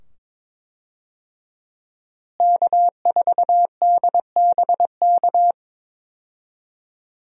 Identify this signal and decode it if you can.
Morse code, heard as a keyed tone, about 22 wpm, 700 Hz. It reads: K4DBK
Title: USA Call Signs - 5 Characters - Repeated 2x - 22wpm